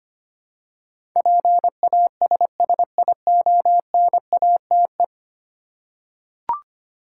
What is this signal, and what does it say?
Morse code, heard as a keyed tone, about 25 wpm, 690 Hz. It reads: PASSIONATE
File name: 10 Letter Words - First 3 Letters Given - 25wpm